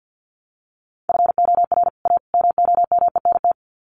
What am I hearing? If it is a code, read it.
Morse code, heard as a keyed tone, about 50 wpm, 700 Hz. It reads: FOR A MOMENT
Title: Sets of 3 Words 50wpm